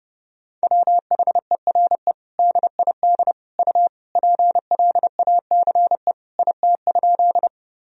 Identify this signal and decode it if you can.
Morse code, heard as a keyed tone, about 30 wpm, 700 Hz. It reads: WHERE DID U PLACE IT?
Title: Sentences from Top 200 Words and Abbreviations 30wpm